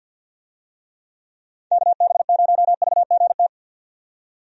Morse code, heard as a keyed tone, about 52 wpm, 700 Hz. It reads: KB0VGT
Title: USA Call Signs - 6 Characters - 50wpm